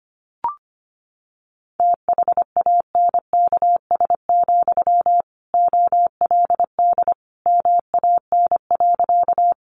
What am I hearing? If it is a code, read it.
Morse code, heard as a keyed tone, about 25 wpm, 700 Hz. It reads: THANKS, OLD MAN.
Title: Sentences from Top 600 Words 25wpm